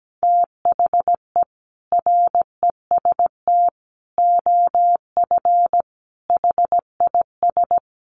Morse code, heard as a keyed tone, about 17 wpm, 700 Hz. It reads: THE REST OF HIS
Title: Sets of 5 Words 17wpm